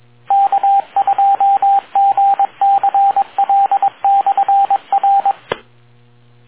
Morse code, heard as a keyed tone, about 22 wpm, 790 Hz, with radio noise. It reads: K2GCL/R